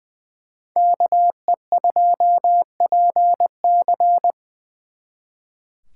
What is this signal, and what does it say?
Morse code, heard as a keyed tone, about 24 wpm, 700 Hz. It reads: KE2PC
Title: Speed Racing - USA 2x2 Call Signs 20wpm